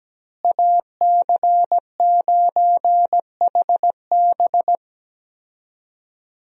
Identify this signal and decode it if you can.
Morse code, heard as a keyed tone, about 17 wpm, 700 Hz. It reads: AC9HB
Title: USA Call Signs - 5 Characters - Repeated 3x - 17wpm